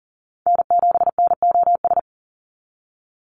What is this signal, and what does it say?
Morse code, heard as a keyed tone, about 40 wpm, 690 Hz. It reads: N7NOS
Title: USA Call Signs 40wpm